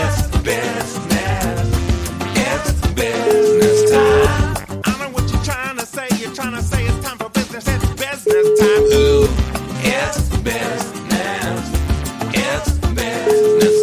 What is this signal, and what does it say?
Ringback tone (Europe, 425 Hz), with noise.